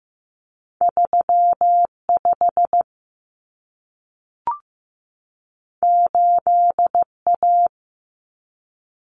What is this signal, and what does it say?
Morse code, heard as a keyed tone, about 15 wpm, 700 Hz. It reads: 35 8A
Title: Common Cut Numbers - 2-Digit 15wpm